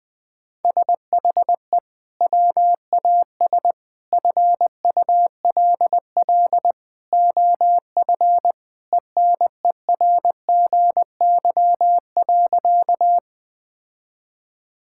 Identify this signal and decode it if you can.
Morse code, heard as a keyed tone, about 21 wpm, 700 Hz. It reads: SHE WAS FULL OF ENERGY.
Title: Sentences from Top 1000 Words - Review Part 6 - 20wpm